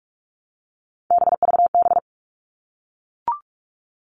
Morse code, heard as a keyed tone, about 44 wpm, 700 Hz. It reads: BVB
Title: All Cut Numbers - 3-Digit 45wpm